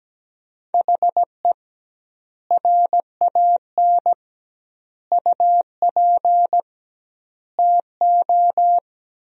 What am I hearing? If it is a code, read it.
Morse code, heard as a keyed tone, about 17 wpm, 700 Hz. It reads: HE RAN UP TO
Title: 2x Word-Spacing - Sentences from Top 500 Words - 17wpm